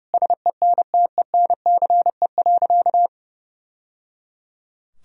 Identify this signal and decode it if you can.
Morse code, heard as a keyed tone, about 30 wpm, 700 Hz. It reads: SENTENCE.
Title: Sentences from Top 1000 Words - Review Part 1 - 30wpm